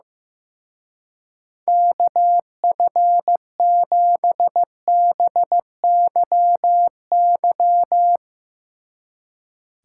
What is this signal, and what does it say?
Morse code, heard as a keyed tone, about 16 wpm, 700 Hz. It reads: KF7BYY
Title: Speed Racing - USA 2x3 Call Signs 15wpm